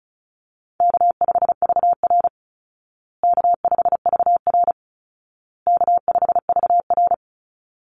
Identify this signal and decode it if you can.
Morse code, heard as a keyed tone, about 34 wpm, 700 Hz. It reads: K5VR K5VR K5VR
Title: Call Signs Repeated 2x - 35wpm